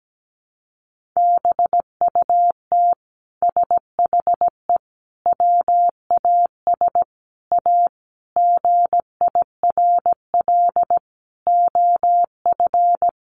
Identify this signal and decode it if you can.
Morse code, heard as a keyed tone, about 17 wpm, 700 Hz. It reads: BUT SHE WAS A GIRL OF
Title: Sentences from Top 1400 Words 17wpm